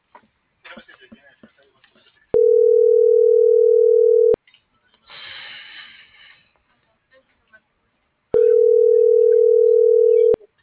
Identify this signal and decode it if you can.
Ringback tone (North America), 440 + 480 Hz.
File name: CENTRAL SOTOVA / out alejandro.rodriguez - .56481